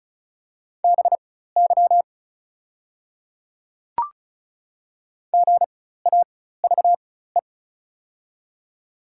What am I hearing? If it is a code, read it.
Morse code, heard as a keyed tone, about 36 wpm, spaced as for 16 wpm, 690 Hz. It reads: BY GAVE
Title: 4x Character Spacing - Top 500 Words - 35wpm